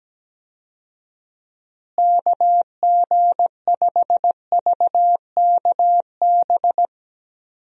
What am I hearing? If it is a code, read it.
Morse code, heard as a keyed tone, about 17 wpm, 700 Hz. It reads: KG5VKB
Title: USA Call Signs - 6 Characters - Repeated 3x - 17wpm